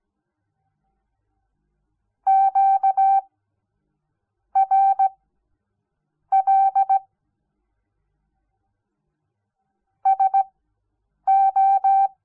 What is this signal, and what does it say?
Morse code, heard as a keyed tone, about 17 wpm, spaced as for 6 wpm, 780 Hz. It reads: QRL SO
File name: Onlinelehrgang / morsen / DJ4UF Morsen06